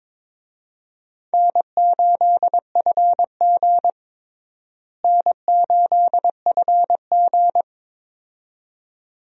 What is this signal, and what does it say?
Morse code, heard as a keyed tone, about 22 wpm, 700 Hz. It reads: N8FG N8FG
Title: USA Call Signs - 4 Characters - Repeated 2x - 22wpm